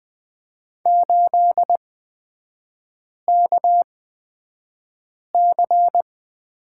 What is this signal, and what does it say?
Morse code, heard as a keyed tone, about 20 wpm, spaced as for 5 wpm, 700 Hz. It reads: 8KC
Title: 255 Lesson 41e - Period - Three Characters 20wpm